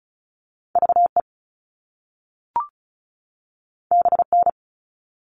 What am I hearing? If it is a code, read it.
Morse code, heard as a keyed tone, about 36 wpm, 690 Hz. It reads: VE BN
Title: All Cut Numbers - 2-Digit 35wpm